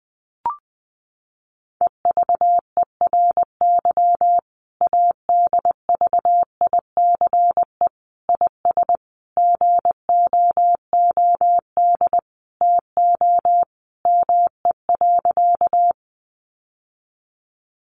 Morse code, heard as a keyed tone, about 20 wpm, 700 Hz. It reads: EVERY ADVICE IS GOOD TO ME.
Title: Sentences from Top 1100 Words 20wpm